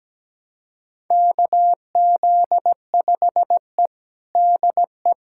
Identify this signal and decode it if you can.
Morse code, heard as a keyed tone, about 17 wpm, 700 Hz. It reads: KZ5E DE
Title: USA Callsign DE Callsign - 4 Characters - 17wpm